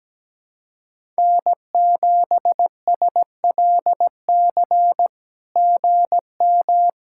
Morse code, heard as a keyed tone, about 17 wpm, 700 Hz. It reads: N7SLC GM
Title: SOTA QSO Practice 17wpm